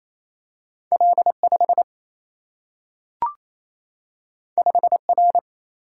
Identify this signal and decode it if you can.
Morse code, heard as a keyed tone, about 28 wpm, 700 Hz. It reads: L5 5R
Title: Single-Letter Single-Number 28wpm